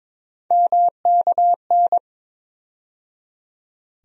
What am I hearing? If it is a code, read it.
Morse code, heard as a keyed tone, about 22 wpm, 700 Hz. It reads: MKN
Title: USA Call Sign Postfixes - 3 Letters 22wpm